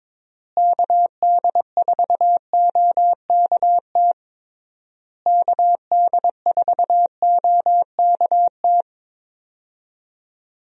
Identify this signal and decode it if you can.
Morse code, heard as a keyed tone, about 21 wpm, 700 Hz. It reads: KD4OKT KD4OKT
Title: USA Call Signs - 6 Characters - Repeated 3x - 22wpm